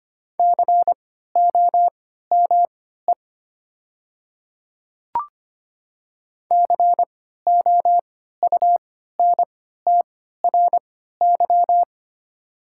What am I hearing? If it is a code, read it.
Morse code, heard as a keyed tone, about 25 wpm, spaced as for 14 wpm, 700 Hz. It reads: COME COUNTRY
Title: 3x Character Spacing - Top 300 Words - 25wpm